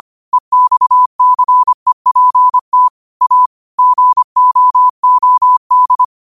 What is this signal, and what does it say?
Morse code, heard as a keyed tone, about 25 wpm, 1000 Hz. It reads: EXCEPT A GOOD
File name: qotdfiles / 25wpm